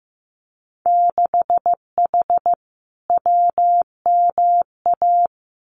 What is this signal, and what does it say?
Morse code, heard as a keyed tone, about 15 wpm, 700 Hz. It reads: 6H WMA
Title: Winter Field Day Exchanges Encore 15wpm